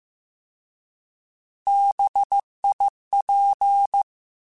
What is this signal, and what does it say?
Morse code, heard as a keyed tone, about 15 wpm, 800 Hz. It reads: BIP
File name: Alphabetical / Alphabetical mp3s / Bs @13